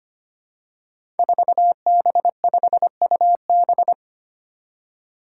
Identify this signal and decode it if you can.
Morse code, heard as a keyed tone, about 25 wpm, 700 Hz. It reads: 4B5UB
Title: Five Challenging Letter-Number Combinations 25wpm